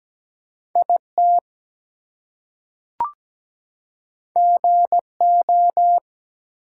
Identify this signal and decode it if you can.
Morse code, heard as a keyed tone, about 17 wpm, 700 Hz. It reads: IT GO
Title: Two Letter Words - 17wpm